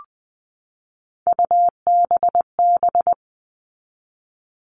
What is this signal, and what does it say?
Morse code, heard as a keyed tone, about 20 wpm, 700 Hz. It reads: UBB
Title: All Cut Numbers - 3-Digit 20wpm